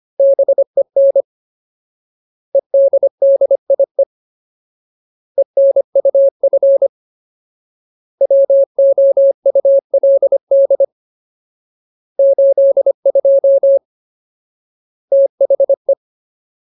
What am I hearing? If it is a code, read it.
Morse code, heard as a keyed tone, about 25 wpm, 550 Hz. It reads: BEN EDDIE ENUF WOULD 82 THE
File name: Ultimate Word List / 25WPM Ultimate